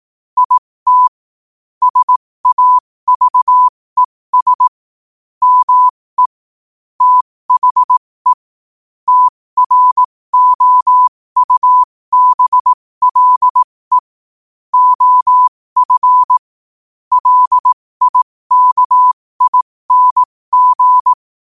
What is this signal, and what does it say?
Morse code, heard as a keyed tone, about 18 wpm, spaced as for 15 wpm, 1000 Hz. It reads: IT SAVES ME THE TROUBLE OF LIKING
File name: qotdfiles / 15wpm